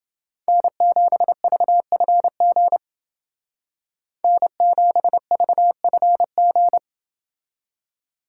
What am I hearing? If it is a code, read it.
Morse code, heard as a keyed tone, about 29 wpm, 700 Hz. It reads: N7VFG N7VFG